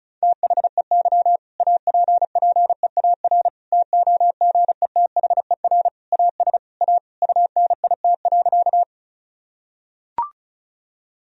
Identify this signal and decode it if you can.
Morse code, heard as a keyed tone, about 36 wpm, 700 Hz. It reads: THEY APPEAR TOGETHER AS A UNIT.
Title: Sentences from Top 1000 Words - Review Part 2 - 35wpm